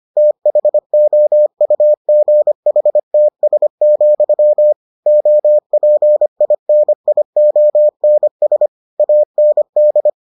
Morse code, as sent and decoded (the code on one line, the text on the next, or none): - .... --- ..- --. .... - ... --..-- / --- .--. .. -. .. --- -. ... / .- -. -..
THOUGHTS, OPINIONS AND